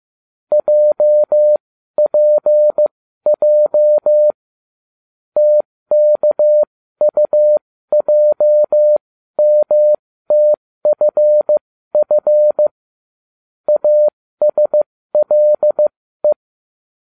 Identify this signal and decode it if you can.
Morse code, heard as a keyed tone, about 15 wpm, spaced as for 13 wpm, 600 Hz. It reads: JPJ TKUJMTFF ASLE